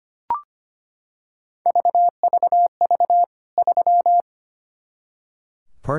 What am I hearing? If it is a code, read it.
Morse code, heard as a keyed tone, about 25 wpm, 700 Hz. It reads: VVV 3
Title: K1USN SST QSO Practice 25wpm